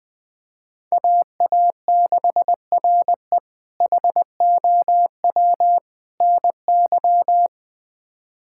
Morse code, heard as a keyed tone, about 20 wpm, 700 Hz. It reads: AA6RE HOW NY
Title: K1USN SST QSO Practice - 20wpm